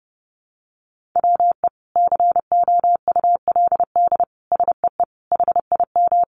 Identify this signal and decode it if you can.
Morse code, heard as a keyed tone, about 30 wpm, 700 Hz. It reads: WE COULD SEE HIM
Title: Sentences from Top 1000 Words 30wpm